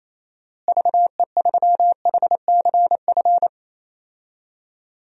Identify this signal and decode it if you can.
Morse code, heard as a keyed tone, about 28 wpm, 700 Hz. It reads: VE3HCF
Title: DXCC-100 Call Signs - 6 Characters - Repeated 3x - 28wpm